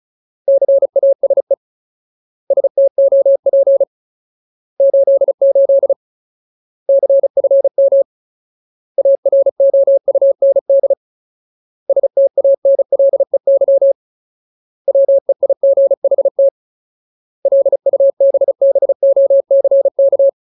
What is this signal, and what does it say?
Morse code, heard as a keyed tone, about 34 wpm, 550 Hz. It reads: CASE STOP 88 CFM AROUND STANLEY WEIGHT LUBBOCK
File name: Ultimate Word List / 35WPM Ultimate